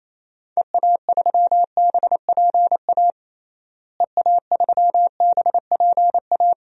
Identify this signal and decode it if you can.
Morse code, heard as a keyed tone, about 28 wpm, 700 Hz. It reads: EA3BPA EA3BPA
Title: DXCC-100 Call Signs - 6 Characters - Repeated 2x - 28wpm